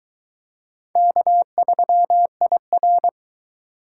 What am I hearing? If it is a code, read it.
Morse code, heard as a keyed tone, about 22 wpm, 700 Hz. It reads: K3IR